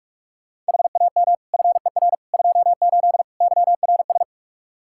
Morse code, heard as a keyed tone, about 46 wpm, 700 Hz. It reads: HAM FER 28 YRS
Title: K1USN SST QSO Practice 45wpm